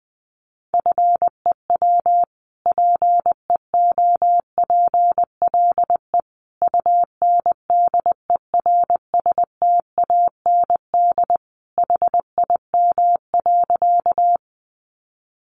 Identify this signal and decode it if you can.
Morse code, heard as a keyed tone, about 19 wpm, 700 Hz. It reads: FEW PEOPLE UNDERSTAND HIM.
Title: Sentences from Top 600 Words 20wpm